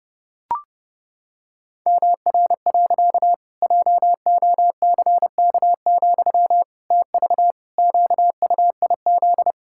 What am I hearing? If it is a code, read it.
Morse code, heard as a keyed tone, about 32 wpm, 700 Hz. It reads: MR. JOCK, TV QUIZ
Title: Pangram Challenge v2 - 30wpm